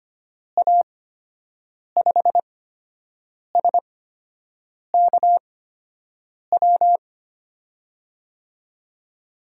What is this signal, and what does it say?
Morse code, heard as a keyed tone, about 25 wpm, spaced as for 7 wpm, 700 Hz. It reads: A5SKW
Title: USA Call Signs - 8x Character Spacing - 25wpm